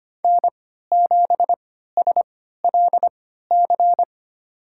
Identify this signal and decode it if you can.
Morse code, heard as a keyed tone, about 25 wpm, spaced as for 14 wpm, 700 Hz. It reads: N7SLC
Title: USA Call Signs - 5 Characters - 3x Character Spacing - 25wpm